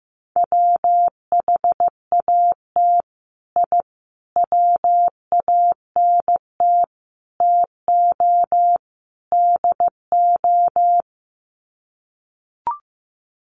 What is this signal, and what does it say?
Morse code, heard as a keyed tone, about 14 wpm, 700 Hz. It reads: WHAT I WANT TO DO
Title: Sets of 5 Words Encore 15wpm